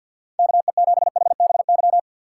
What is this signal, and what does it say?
Morse code, heard as a keyed tone, about 53 wpm, 700 Hz. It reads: KE7HBY
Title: USA Call Signs - 2x Character Spacing - 50wpm